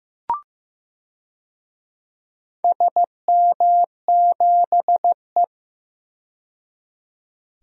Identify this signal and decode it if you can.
Morse code, heard as a keyed tone, about 15 wpm, 700 Hz. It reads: SM7E
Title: DXCC-100 Call Signs - 4 Characters - 15wpm